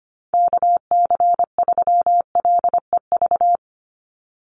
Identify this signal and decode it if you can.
Morse code, heard as a keyed tone, about 25 wpm, 690 Hz. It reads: KC3LEV
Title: USA Call Signs 25wpm